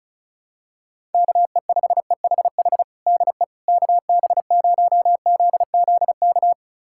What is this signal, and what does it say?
Morse code, heard as a keyed tone, about 36 wpm, 700 Hz. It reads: KE5EHH DE KB0ZZK
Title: USA Callsign DE Callsign - 6 Characters - 35wpm